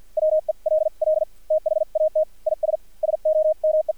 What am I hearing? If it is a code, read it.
Morse code, heard as a keyed tone, about 53 wpm, spaced as for 44 wpm, 620 Hz. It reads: WELL THAT IS SOME
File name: archived / CRSnet